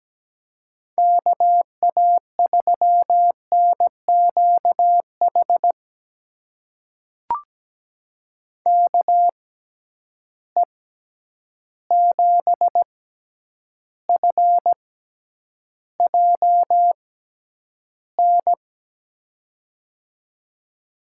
Morse code, heard as a keyed tone, about 17 wpm, spaced as for 9 wpm, 700 Hz. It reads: KA3NQH KE7FJN
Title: USA Call Signs - 6 Characters - 6x Character Spacing - 17wpm